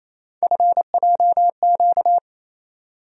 Morse code, heard as a keyed tone, about 28 wpm, 700 Hz. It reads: FJQ
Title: USA Call Sign Postfixes - 3 Letters 28wpm